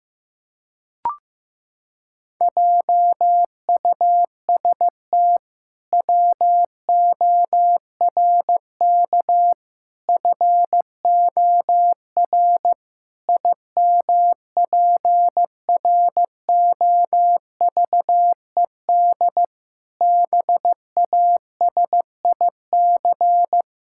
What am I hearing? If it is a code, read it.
Morse code, heard as a keyed tone, about 16 wpm, 700 Hz. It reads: JUST WORK FOR IMPROVED BASIC
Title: Pangram Challenge v2 - 15wpm